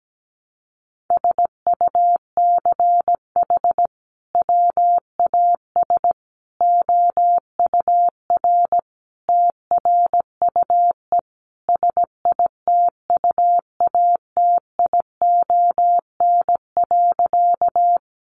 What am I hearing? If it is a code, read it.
Morse code, heard as a keyed tone, about 17 wpm, 700 Hz. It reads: SUCH WAS OUR TRUE SITUATION.
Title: Sentences from Top 1200 Words 17wpm